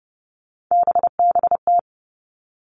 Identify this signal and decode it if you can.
Morse code, heard as a keyed tone, about 32 wpm, 700 Hz. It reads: BBT
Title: Three Challenging Letter-Number Combinations 30wpm